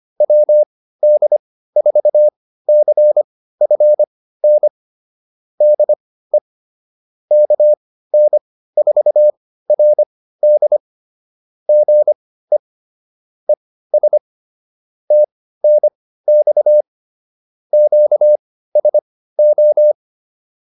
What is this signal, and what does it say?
Morse code, heard as a keyed tone, about 25 wpm, spaced as for 15 wpm, 600 Hz. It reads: WD4CFN DE KN4RD GE ES TNX QSO